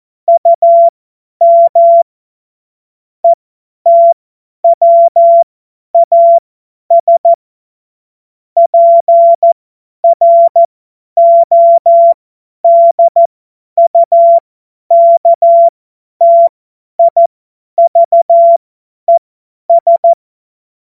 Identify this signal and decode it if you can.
Morse code, heard as a keyed tone, about 14 wpm, spaced as for 10 wpm, 680 Hz. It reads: UM ETWAS PRODUKTIVES
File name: Morsekurs / Witze / Witz 29. Juni 2021, 50 F70